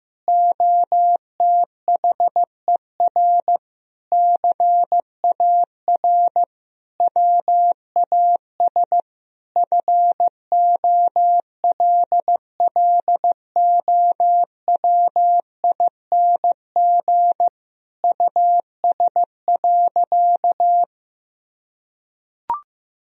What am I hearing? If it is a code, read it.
Morse code, heard as a keyed tone, about 16 wpm, 700 Hz. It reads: OTHER CAR WAS FOLLOWING US.
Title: Sentences from Top 1000 Words - Review Part 4 - 15wpm